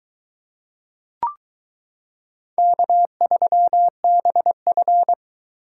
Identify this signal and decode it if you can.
Morse code, heard as a keyed tone, about 22 wpm, 700 Hz. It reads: K3BF